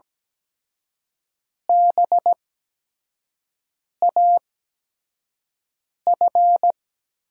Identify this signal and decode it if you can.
Morse code, heard as a keyed tone, about 17 wpm, spaced as for 5 wpm, 700 Hz. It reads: BAF
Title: USA Call Signs - 5 Characters - 8x Character Spacing - 17wpm